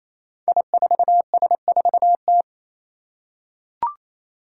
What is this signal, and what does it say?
Morse code, heard as a keyed tone, about 27 wpm, 700 Hz. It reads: I4S4T
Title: Five Challenging Letter-Number Combinations 28wpm